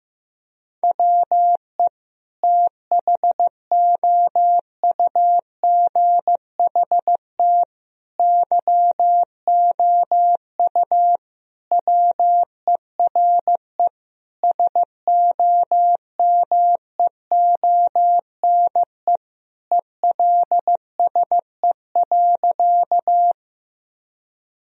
Morse code, heard as a keyed tone, about 15 wpm, 700 Hz. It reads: WE THOUGHT YOU WERE SOMEONE ELSE.
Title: Sentences from Top 1000 Words - Review Part 4 - 15wpm